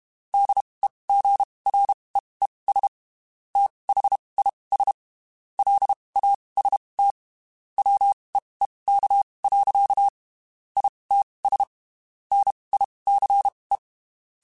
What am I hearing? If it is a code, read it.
Morse code, heard as a keyed tone, about 32 wpm, spaced as for 23 wpm, 800 Hz. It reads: DEGREES THIS LAST WEEK. ITS NICE
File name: (spaced) / QSOs Spaced-mp3s / WD0FBT @30 sp